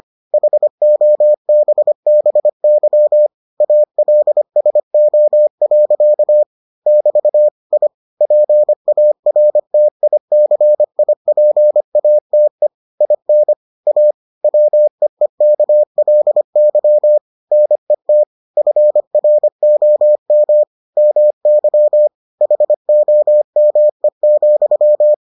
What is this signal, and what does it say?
Morse code, heard as a keyed tone, about 25 wpm, 600 Hz. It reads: HOBBY ALSO. <BT> I PARTICIPATE IN A WEEKLY NET FROM MY HOME,